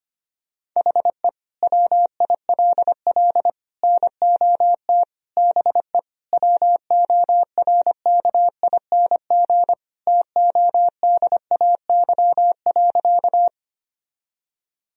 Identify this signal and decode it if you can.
Morse code, heard as a keyed tone, about 25 wpm, 700 Hz. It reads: HE WILL NOT BE WORKING TODAY.
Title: Sentences from Top 1000 Words - Review Part 4 - 25wpm